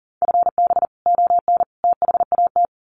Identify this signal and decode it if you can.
Morse code, heard as a keyed tone, about 40 wpm, 700 Hz. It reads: FB ON THAT
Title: Common QSO Phrases 40wpm